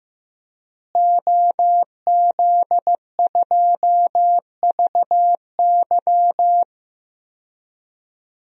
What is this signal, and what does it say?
Morse code, heard as a keyed tone, about 15 wpm, 700 Hz. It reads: OZ2VY
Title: DXCC-100 Call Signs - 5 Characters - Repeated 3x - 15wpm